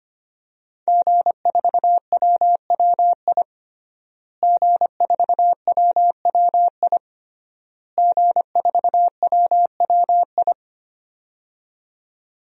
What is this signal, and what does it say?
Morse code, heard as a keyed tone, about 25 wpm, 700 Hz. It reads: G4WWI G4WWI G4WWI
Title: DXCC-100 Call Signs - 5 Characters - Repeated 3x - 25wpm